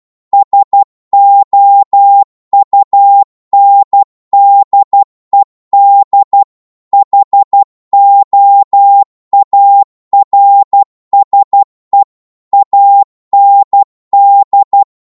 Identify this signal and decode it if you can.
Morse code, heard as a keyed tone, about 12 wpm, 800 Hz. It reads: SOUNDED HOARSE AND